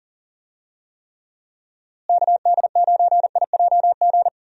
Morse code, heard as a keyed tone, about 40 wpm, 700 Hz. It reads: KD9IJG
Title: USA Call Signs - 6 Characters - Repeated 3x - 40wpm